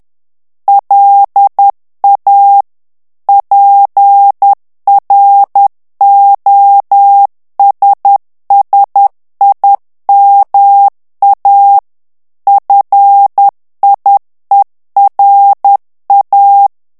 Morse code, heard as a keyed tone, about 10 wpm, 800 Hz. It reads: LA PROSSIMA FIERA
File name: morse mp3 / ITALIEN texte clair 1 vitesse 10 mots minute